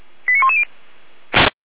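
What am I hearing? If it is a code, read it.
Five-tone selective call (ZVEI), heard as quick runs of five tones, tones of about 70 ms. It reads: 89110
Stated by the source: ZVEI five-tone call received by SDR, from feuerwehr-orlamuende.org.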